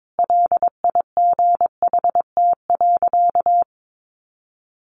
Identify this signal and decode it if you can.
Morse code, heard as a keyed tone, about 22 wpm, 700 Hz. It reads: LIGHT.
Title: Sentences from Top 200 Words 22wpm